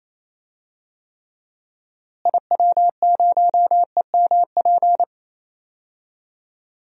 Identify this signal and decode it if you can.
Morse code, heard as a keyed tone, about 28 wpm, 700 Hz. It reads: IW0EMP
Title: DXCC-100 Call Signs - 6 Characters - Repeated 3x - 28wpm